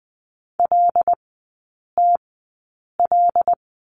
Morse code, heard as a keyed tone, about 20 wpm, spaced as for 8 wpm, 700 Hz. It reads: LTL
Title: 074 Lesson 13e - Letter L - Three Characters 20wpm